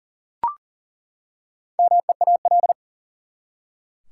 Morse code, heard as a keyed tone, about 40 wpm, 700 Hz. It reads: MEAL
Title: Four Letter Words - 40wpm